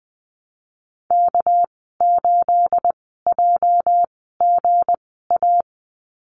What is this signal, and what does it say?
Morse code, heard as a keyed tone, about 20 wpm, spaced as for 14 wpm, 700 Hz. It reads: K8JGA
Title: 245 Lesson 39g - Letter z - Call Signs 20wpm